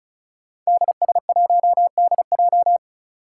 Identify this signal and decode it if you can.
Morse code, heard as a keyed tone, about 36 wpm, 700 Hz. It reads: DS1DJ
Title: Five Challenging Letter-Number Combinations 35wpm